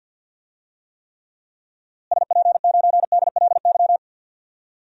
Morse code, heard as a keyed tone, about 52 wpm, 700 Hz. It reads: SP9DLY
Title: DXCC-100 Call Signs - 6 Characters - Repeated 3x - 50wpm